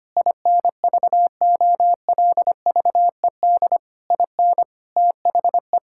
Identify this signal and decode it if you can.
Morse code, heard as a keyed tone, about 25 wpm, 700 Hz. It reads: INVOLVED IN THE
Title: Sentences from Top 1000 Words - Review Part 5 - 25wpm